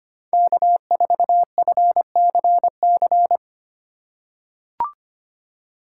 Morse code, heard as a keyed tone, about 25 wpm, 700 Hz. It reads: K4FCC
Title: K1USN SST QSO Practice - 25wpm